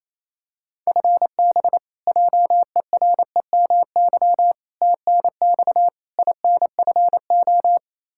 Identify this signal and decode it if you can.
Morse code, heard as a keyed tone, about 28 wpm, 700 Hz. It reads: FB JEREMY TNX INFO
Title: Practice QSO - Exchange 3 - More Info - 28wpm